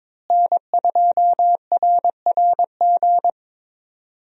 Morse code, heard as a keyed tone, about 22 wpm, 700 Hz. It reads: N2RRG
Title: USA Call Signs - 5 Characters - Repeated 3x - 22wpm